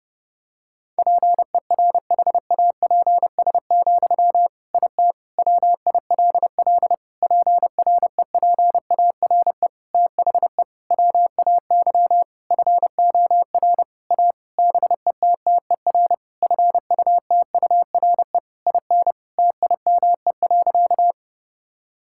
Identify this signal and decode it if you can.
Morse code, heard as a keyed tone, about 30 wpm, 700 Hz. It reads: PERHAPS, IT WILL PREPARE THE WAY FOR A BETTER FUTURE IN TIME.
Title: Sentences from Top 1000 Words - Review Part 6 - 30wpm